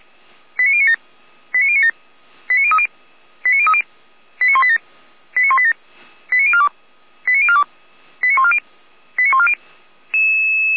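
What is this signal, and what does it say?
Five-tone selective call (ZVEI), heard as quick runs of five tones, tones of about 70 ms. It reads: 89097 89097 89030 89030 89187 89187 89042 89042 89140 89140
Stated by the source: ZVEI five-tone selective call received on an SDR.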